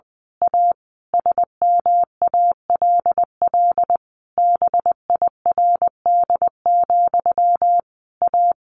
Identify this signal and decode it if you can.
Morse code, heard as a keyed tone, about 20 wpm, 700 Hz. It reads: A SMALL BIRD, A
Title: A Sound Of Thunder 20wpm